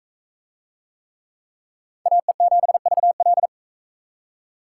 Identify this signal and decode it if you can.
Morse code, heard as a keyed tone, about 41 wpm, 700 Hz. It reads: AE7VL